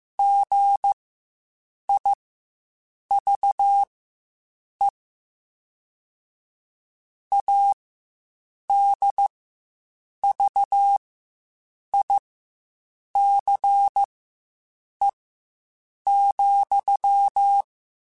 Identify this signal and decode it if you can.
Morse code, heard as a keyed tone, about 15 wpm, spaced as for 7 wpm, 800 Hz. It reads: GIVE ADVICE,
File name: Quotes Spaced